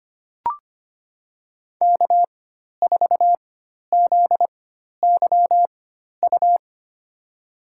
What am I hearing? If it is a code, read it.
Morse code, heard as a keyed tone, about 25 wpm, spaced as for 12 wpm, 700 Hz. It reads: K4ZYU
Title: USA Call Signs - 5 Characters - 4x Character Spacing - 25wpm